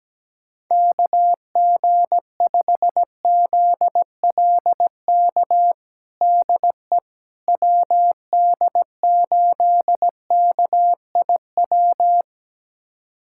Morse code, heard as a keyed tone, about 17 wpm, 700 Hz. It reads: KG5ZLK DE WD8KIW
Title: USA Callsign DE Callsign - 6 Characters - 17wpm